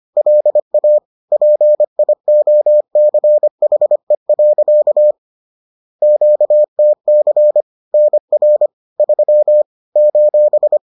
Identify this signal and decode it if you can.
Morse code, heard as a keyed tone, about 25 wpm, 600 Hz. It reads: LA PIOCHE. QTC NR 3 :
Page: bulletin UFT sous forme texte et en CW